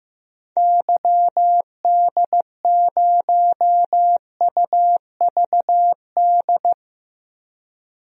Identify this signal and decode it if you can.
Morse code, heard as a keyed tone, about 16 wpm, 700 Hz. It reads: YD0UVD
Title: DXCC-100 Call Signs - 6 Characters - 15wpm